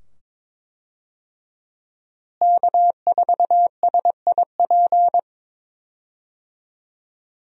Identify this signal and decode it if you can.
Morse code, heard as a keyed tone, about 22 wpm, 700 Hz. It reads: K4SIP
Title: USA Call Signs - 5 Characters - Repeated 2x - 22wpm